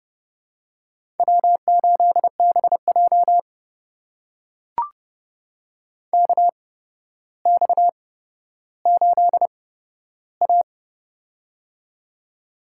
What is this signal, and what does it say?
Morse code, heard as a keyed tone, about 30 wpm, spaced as for 13 wpm, 700 Hz. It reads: W8BJ KX8A
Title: USA Call Signs - 4 Characters - 8x Character Spacing - 30wpm